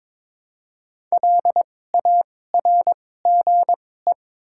Morse code, heard as a keyed tone, about 21 wpm, spaced as for 16 wpm, 700 Hz. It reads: LARGE
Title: 2x Character Spacing - Top 200 Words - 22wpm